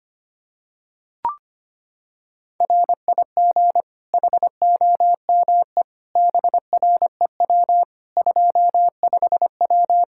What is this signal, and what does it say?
Morse code, heard as a keyed tone, about 25 wpm, 700 Hz. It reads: RIG HOME BREW 25W
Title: K1USN SST QSO Practice 25wpm